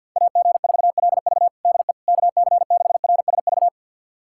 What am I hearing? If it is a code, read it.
Morse code, heard as a keyed tone, about 51 wpm, 700 Hz. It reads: AG4LV DE KC6RSV
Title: USA Callsign DE Callsign 50wpm